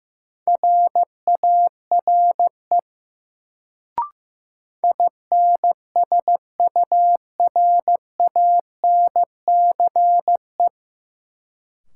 Morse code, heard as a keyed tone, about 15 wpm, 700 Hz. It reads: RARE INSURANCE
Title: Sentences from Top 1100 Words - New Word Drill - 15wpm